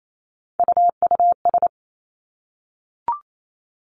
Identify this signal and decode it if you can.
Morse code, heard as a keyed tone, about 28 wpm, 700 Hz. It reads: UUS
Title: Three Challenging Letter-Number Combinations 28wpm